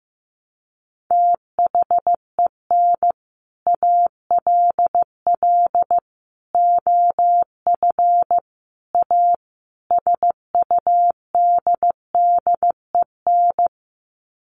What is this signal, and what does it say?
Morse code, heard as a keyed tone, about 15 wpm, 700 Hz. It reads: THEN ALL OF A SUDDEN
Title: Sets of 5 Words Encore 15wpm